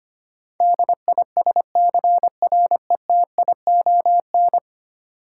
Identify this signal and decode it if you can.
Morse code, heard as a keyed tone, about 25 wpm, 700 Hz. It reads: DISCRETION
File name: Ten Letter Words - 25wpm